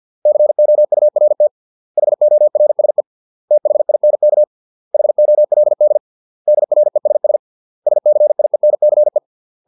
Morse code, heard as a keyed tone, about 50 wpm, 600 Hz. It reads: COURT HORSE THINK HOLD DRESS SCIENCE